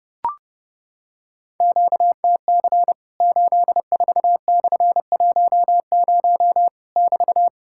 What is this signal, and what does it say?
Morse code, heard as a keyed tone, about 30 wpm, 700 Hz. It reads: QTC 84/10 <BT>
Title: Worked All Europe DX Contest - QTC Exchange - 30wpm